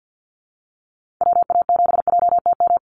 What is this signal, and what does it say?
Morse code, heard as a keyed tone, about 50 wpm, 700 Hz. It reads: WA7JTM
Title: NA SOTA Chaser Callsigns 50wpm